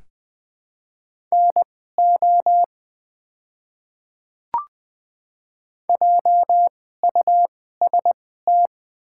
Morse code, heard as a keyed tone, about 20 wpm, spaced as for 14 wpm, 700 Hz. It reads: NO JUST
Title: 2x Character Spacing - Top 300 Words - 20wpm